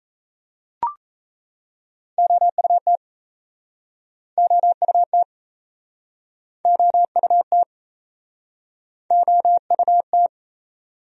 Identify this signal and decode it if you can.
Morse code, heard as a keyed tone, about 40 wpm, 700 Hz. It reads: OUT OUT OUT OUT